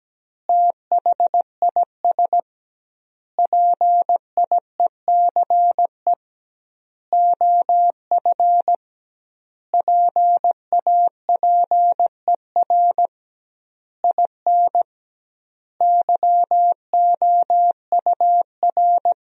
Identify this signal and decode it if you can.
Morse code, heard as a keyed tone, about 17 wpm, 700 Hz. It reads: THIS PIECE OF PAPER IN YOUR
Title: 2x Word-Spacing - Sentences from Top 400 Words - 17wpm